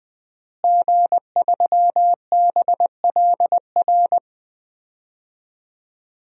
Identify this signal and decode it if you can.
Morse code, heard as a keyed tone, about 20 wpm, 700 Hz. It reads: G3BLR
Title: DXCC-100 Call Signs - 5 Characters - Repeated 3x - 20wpm